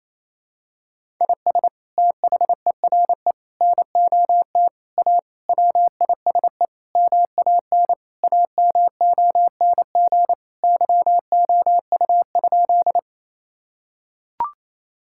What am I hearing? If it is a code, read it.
Morse code, heard as a keyed tone, about 28 wpm, 700 Hz. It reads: IS THERE NOT A WISE MAN AMONG YOU?
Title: Sentences from Top 1600 Words 28wpm